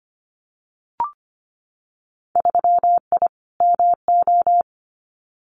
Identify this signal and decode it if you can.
Morse code, heard as a keyed tone, about 25 wpm, 700 Hz. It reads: 3I MO
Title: Winter Field Day Exchanges 25wpm